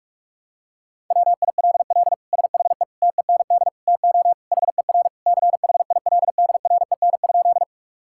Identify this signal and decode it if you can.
Morse code, heard as a keyed tone, about 42 wpm, 700 Hz. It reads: WILL SHE TEND TO HER CHILDREN?
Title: Sentences from Top 1000 Words - Review Part 5 - 45wpm